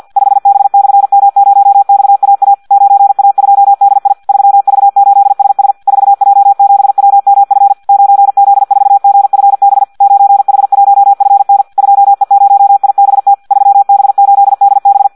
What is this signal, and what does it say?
Morse code, heard as a keyed tone, about 50 wpm, 800 Hz. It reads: 5B/M0XAA 9A2DI 3V8SS 4J7WMF 9Z4GFB 9H1FN 2E0IBT 3B8DB
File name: Audio practica CW / qrq-tx-50 2